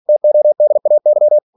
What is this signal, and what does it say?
Morse code, heard as a keyed tone, about 46 wpm, 600 Hz. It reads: TODAY
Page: Never 47wpm